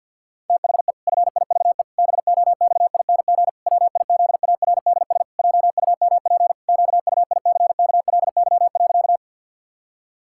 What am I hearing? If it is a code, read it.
Morse code, heard as a keyed tone, about 51 wpm, 700 Hz. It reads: THE FIVE BOXING WIZARDS JUMP QUICKLY.
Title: Pangram Challenge v2 - 50wpm